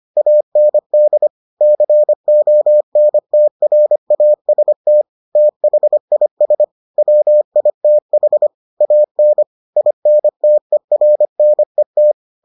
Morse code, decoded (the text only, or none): AND CONTRAST THIS WITH AN INTERNET